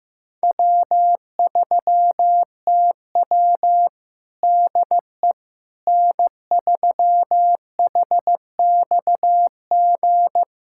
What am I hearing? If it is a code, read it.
Morse code, heard as a keyed tone, about 15 wpm, 700 Hz. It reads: W3TW DE N3HXG
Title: Practice QSO - Exchange 4 - Ending - 15wpm